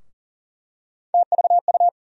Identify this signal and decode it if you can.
Morse code, heard as a keyed tone, about 40 wpm, 700 Hz. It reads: TVU
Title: USA Call Sign Postfixes - 3 Letters 40wpm